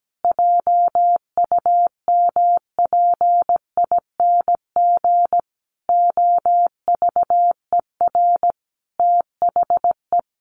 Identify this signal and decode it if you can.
Morse code, heard as a keyed tone, about 17 wpm, 700 Hz. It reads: JUMPING OVER THE
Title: Easy Phrases 17wpm